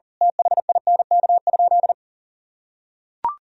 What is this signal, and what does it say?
Morse code, heard as a keyed tone, about 40 wpm, 700 Hz. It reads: THINK?
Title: Sentences from Top 200 Words and Abbreviations 40wpm